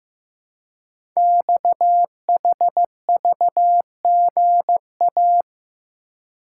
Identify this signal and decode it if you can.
Morse code, heard as a keyed tone, about 15 wpm, 700 Hz. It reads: XHVGA
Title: Code Groups - 5 Letters - 15wpm